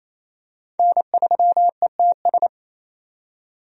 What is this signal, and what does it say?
Morse code, heard as a keyed tone, about 28 wpm, 700 Hz. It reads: N3ETS
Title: USA Call Signs - 5 Characters - 28wpm